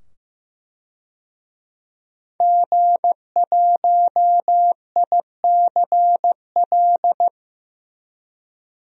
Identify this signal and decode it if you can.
Morse code, heard as a keyed tone, about 15 wpm, 700 Hz. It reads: G1ICL
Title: DXCC-100 Call Signs - 5 Characters - 15wpm